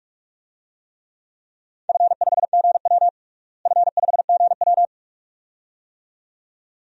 Morse code, heard as a keyed tone, about 46 wpm, 700 Hz. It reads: F5GW F5GW
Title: DXCC-100 Call Signs - 4 Characters - Repeated 2x - 45wpm